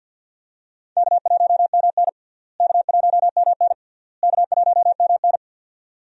Morse code, heard as a keyed tone, about 52 wpm, 700 Hz. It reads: K1MN K1MN K1MN
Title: USA Call Signs - 4 Characters - Repeated 3x - 50wpm